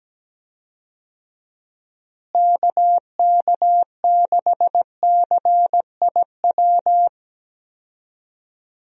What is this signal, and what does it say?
Morse code, heard as a keyed tone, about 17 wpm, 700 Hz. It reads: KK6CIW
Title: USA Call Signs - 6 Characters - Repeated 2x - 17wpm